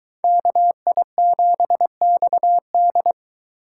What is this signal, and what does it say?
Morse code, heard as a keyed tone, about 22 wpm, 700 Hz. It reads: KI7XD